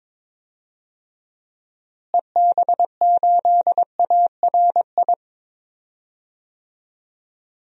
Morse code, heard as a keyed tone, about 22 wpm, 700 Hz. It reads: EB8ARI
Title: DXCC-100 Call Signs - 6 Characters - Repeated 3x - 22wpm